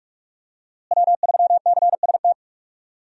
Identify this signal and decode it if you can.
Morse code, heard as a keyed tone, about 42 wpm, 700 Hz. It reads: W3CST
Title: USA Call Signs - 5 Characters - Repeated 2x - 45wpm